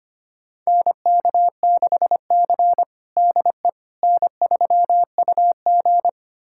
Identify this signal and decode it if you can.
Morse code, heard as a keyed tone, about 25 wpm, 700 Hz. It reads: NK6C DE N3UG
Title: USA Callsign DE Callsign - 4 Characters - 25wpm